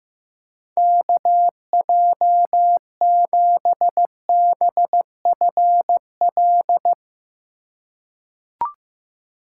Morse code, heard as a keyed tone, about 15 wpm, 700 Hz. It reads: KJ7BFL
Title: USA Call Signs - 6 Characters - 8x Character Spacing - 15wpm